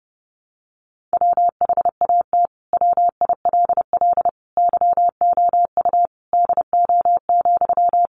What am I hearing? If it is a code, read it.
Morse code, heard as a keyed tone, about 30 wpm, 700 Hz. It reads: WHAT WILL YOU DO,
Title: Sentences from Top 100 Words 30wpm